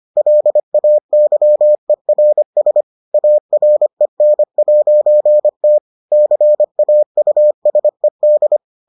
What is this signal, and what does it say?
Morse code, heard as a keyed tone, about 25 wpm, 600 Hz. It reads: LAYERS AREN'T CAUSED